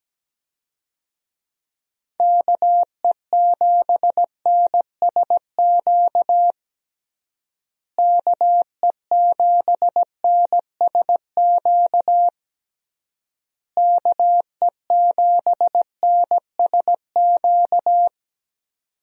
Morse code, heard as a keyed tone, about 17 wpm, 700 Hz. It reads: KE7NSQ KE7NSQ KE7NSQ
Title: USA Call Signs - 6 Characters - Repeated 3x - 17wpm